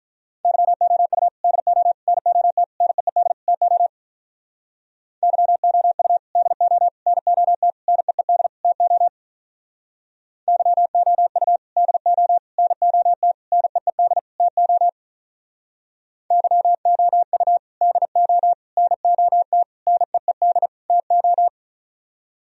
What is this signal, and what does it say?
Morse code, heard as a keyed tone, about 46 wpm, 700 Hz. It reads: YOU DO NOT NEED TO YOU DO NOT NEED TO YOU DO NOT NEED TO YOU DO NOT NEED TO